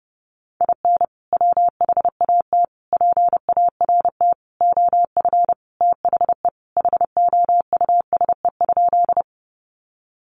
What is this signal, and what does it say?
Morse code, heard as a keyed tone, about 28 wpm, 700 Hz. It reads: IN WHAT PART OF THE HOUSE?
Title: Sentences from Top 200 Words 30wpm